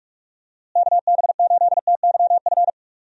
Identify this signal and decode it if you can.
Morse code, heard as a keyed tone, about 42 wpm, 700 Hz. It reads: KB8TYF
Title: USA Call Signs - 6 Characters - Repeated 2x - 45wpm